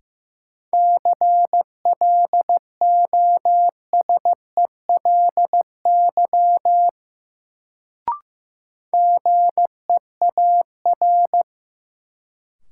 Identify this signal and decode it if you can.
Morse code, heard as a keyed tone, about 15 wpm, 700 Hz. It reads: CLOSELY GEAR
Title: Sentences from Top 1400 Words - New Word Drill - 15wpm